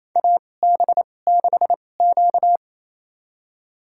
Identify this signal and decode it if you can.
Morse code, heard as a keyed tone, about 28 wpm, spaced as for 20 wpm, 700 Hz. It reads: AB6Q
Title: USA Call Signs - 2x Character Spacing - 28wpm